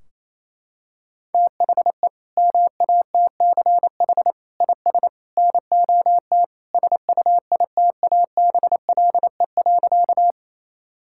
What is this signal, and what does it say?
Morse code, heard as a keyed tone, about 28 wpm, 700 Hz. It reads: THE MATCH IS NOT SUITABLE.
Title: Sentences from Top 1600 Words 28wpm